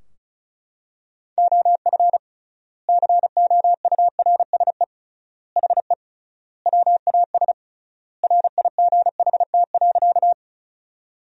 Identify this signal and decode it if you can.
Morse code, heard as a keyed tone, about 34 wpm, 700 Hz. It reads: OF COURSE HE WAS RIGHT.
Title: 3x Word-Spacing - Sentences from Top 500 Words - 35wpm